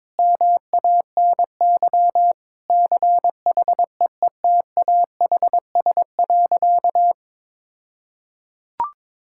Morse code, heard as a keyed tone, about 22 wpm, 690 Hz. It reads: MANY CHEETAHS.
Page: morse-code-ninja-practice-22wpm